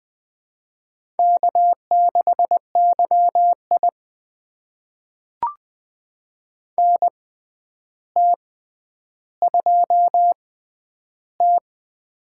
Morse code, heard as a keyed tone, about 20 wpm, spaced as for 10 wpm, 700 Hz. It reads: K6YI NT2T
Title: USA Call Signs - 4 Characters - 6x Character Spacing - 20wpm